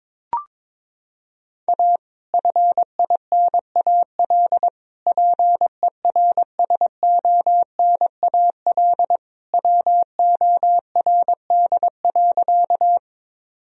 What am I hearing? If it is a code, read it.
Morse code, heard as a keyed tone, about 22 wpm, 700 Hz. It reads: A FINAL PERSONAL WORD.
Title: Sentences from Top 1000 Words - Review Part 4 - 22wpm